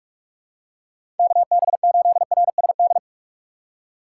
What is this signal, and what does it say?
Morse code, heard as a keyed tone, about 46 wpm, 700 Hz. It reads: KB8RSD
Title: USA Call Signs - 6x Character Spacing - 45wpm